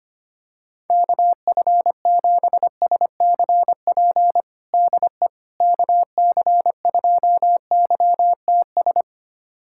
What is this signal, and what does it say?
Morse code, heard as a keyed tone, about 25 wpm, 700 Hz. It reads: KF7SCP DE KC2YTS
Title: USA Callsign DE Callsign - 6 Characters - 25wpm